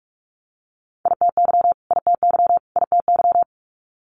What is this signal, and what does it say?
Morse code, heard as a keyed tone, about 46 wpm, 700 Hz. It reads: ITY ITY ITY
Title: Multi-letter Phonemes Repeated 3x 45wpm